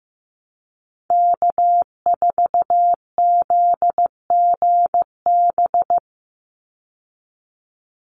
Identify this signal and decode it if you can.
Morse code, heard as a keyed tone, about 15 wpm, 700 Hz. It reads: K4ZGB
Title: Call Signs Repeated 2x - 15wpm